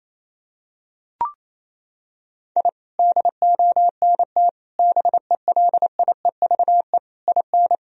Morse code, heard as a keyed tone, about 28 wpm, 700 Hz. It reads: I DONT BELIEVE IN
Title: Sentences from Top 1000 Words - Review Part 2 - 28wpm